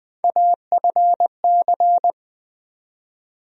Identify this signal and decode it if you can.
Morse code, heard as a keyed tone, about 20 wpm, 700 Hz. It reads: AFC
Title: USA Call Sign Postfixes - 3 Letters 20wpm